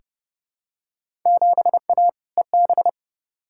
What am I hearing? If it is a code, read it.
Morse code, heard as a keyed tone, about 30 wpm, 700 Hz. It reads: 7A EB
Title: Field Day Exchanges Encore - 30wpm